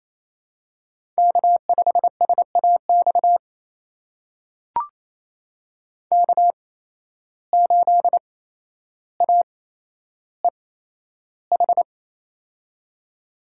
Morse code, heard as a keyed tone, about 27 wpm, spaced as for 12 wpm, 700 Hz. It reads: K5SAX K8AEH
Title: USA Call Signs - 5 Characters - 8x Character Spacing - 28wpm